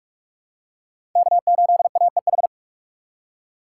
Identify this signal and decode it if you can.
Morse code, heard as a keyed tone, about 47 wpm, 700 Hz. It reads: K8AEH
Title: USA Call Signs - 5 Characters - 2x Character Spacing - 45wpm